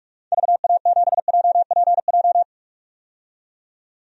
Morse code, heard as a keyed tone, about 42 wpm, 700 Hz. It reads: VA7JPJ
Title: DXCC-100 Call Signs - 6 Characters - 45wpm